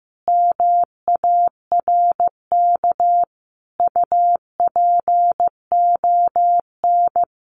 Morse code, heard as a keyed tone, about 15 wpm, 700 Hz. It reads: MARK UPON
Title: Sentences from Top 1100 Words 15wpm